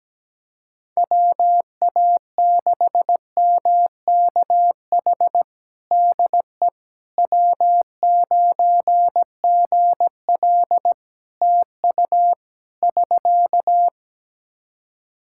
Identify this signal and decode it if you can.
Morse code, heard as a keyed tone, about 17 wpm, 700 Hz. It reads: WA6MKH DE W9GL TU <SK>
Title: Practice QSO - Exchange 4 - Ending - 17wpm